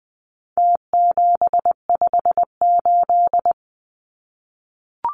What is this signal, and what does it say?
Morse code, heard as a keyed tone, about 20 wpm, 700 Hz. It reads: T758
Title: Numbers - Contest Style - 4-Digit 20wpm